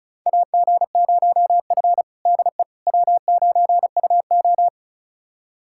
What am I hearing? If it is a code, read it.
Morse code, heard as a keyed tone, about 36 wpm, 700 Hz. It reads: AG0F DE W9UO
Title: USA Callsign DE Callsign - 4 Characters - 35wpm